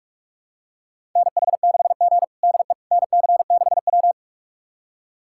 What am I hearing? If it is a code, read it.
Morse code, heard as a keyed tone, about 42 wpm, 700 Hz. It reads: NH6G DE NC6W
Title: USA Callsign DE Callsign - 4 Characters - 45wpm